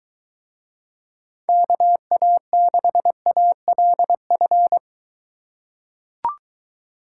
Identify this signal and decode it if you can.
Morse code, heard as a keyed tone, about 22 wpm, 700 Hz. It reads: KA6ALF